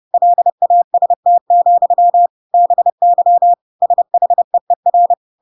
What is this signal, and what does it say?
Morse code, heard as a keyed tone, about 30 wpm, 700 Hz. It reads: LAST, BY SHEER
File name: Pinocchio / Pinocchio ch3 30 30 ik1hge